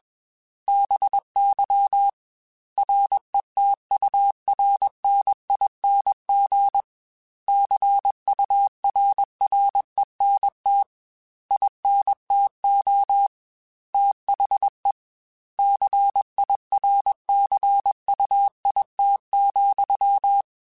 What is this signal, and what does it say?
Morse code, heard as a keyed tone, about 21 wpm, 800 Hz. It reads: BY RETURNING CURRENT INTO THE CIRCUIT,
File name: Technical Subjects / Technical Subjects-mp3s / XC @20